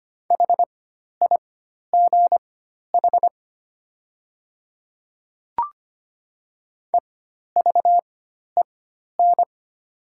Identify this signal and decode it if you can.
Morse code, heard as a keyed tone, about 25 wpm, spaced as for 12 wpm, 700 Hz. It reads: HIGH EVEN
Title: 4x Character Spacing - Top 300 Words - 25wpm